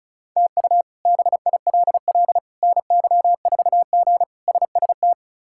Morse code, heard as a keyed tone, about 36 wpm, 700 Hz. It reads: TU BILL NY4G SST
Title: K1USN SST QSO Practice - 35wpm